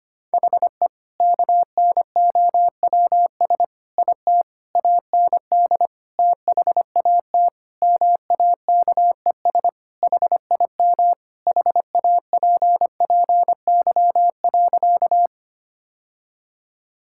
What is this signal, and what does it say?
Morse code, heard as a keyed tone, about 25 wpm, 700 Hz. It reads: HE KNOWS IT AND THAT MAKES HIM HAPPY.
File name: practice / 25wpm / Sentences with 2-Word Phrases - 25wpm